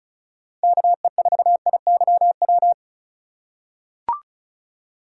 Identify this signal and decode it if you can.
Morse code, heard as a keyed tone, about 36 wpm, 700 Hz. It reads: KE4IYW
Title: USA Call Signs - 6 Characters - 2x Character Spacing - 35wpm